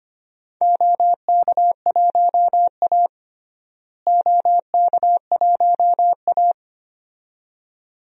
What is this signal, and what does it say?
Morse code, heard as a keyed tone, about 25 wpm, 700 Hz. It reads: OK1A OK1A
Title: DXCC-100 Call Signs - 4 Characters - Repeated 2x - 25wpm